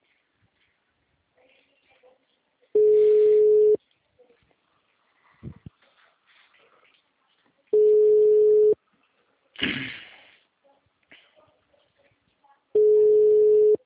Ringback tone (Europe, 425 Hz).